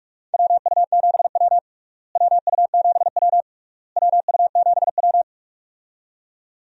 Morse code, heard as a keyed tone, about 46 wpm, 700 Hz. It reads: WU7W WU7W WU7W
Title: USA Call Signs - 4 Characters - Repeated 3x - 45wpm